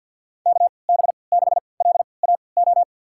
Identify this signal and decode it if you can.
Morse code, heard as a keyed tone, about 53 wpm, spaced as for 29 wpm, 700 Hz. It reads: KB6LAO
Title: USA Call Signs - 6 Characters - 3x Character Spacing - 50wpm